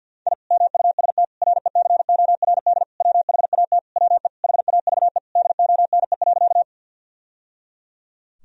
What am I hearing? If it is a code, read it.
Morse code, heard as a keyed tone, about 50 wpm, 700 Hz. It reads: I MUST RECORD WHAT WE HAVE DONE.
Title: Sentences from Top 1000 Words - Review Part 3 - 50wpm